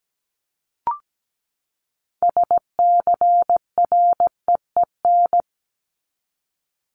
Morse code, heard as a keyed tone, about 17 wpm, 700 Hz. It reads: SCREEN
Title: Top 751-1000 Words 17wpm